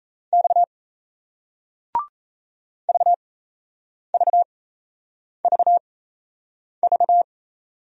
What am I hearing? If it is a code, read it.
Morse code, heard as a keyed tone, about 36 wpm, spaced as for 8 wpm, 700 Hz. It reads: X VVVV